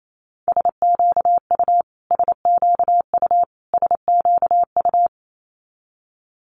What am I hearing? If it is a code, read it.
Morse code, heard as a keyed tone, about 28 wpm, 700 Hz. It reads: SQU SQU SQU
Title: Multi-letter Phonemes Repeated 3x 28wpm